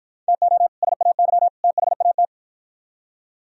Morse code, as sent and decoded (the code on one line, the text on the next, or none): - --- / ... .- -.-- / - .... .- -
TO SAY THAT